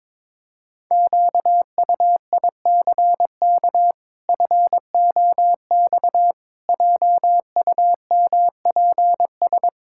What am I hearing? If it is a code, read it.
Morse code, heard as a keyed tone, about 22 wpm, 700 Hz. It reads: QUICK FOX JUMPS
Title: Pangram Challenge v2 - 22wpm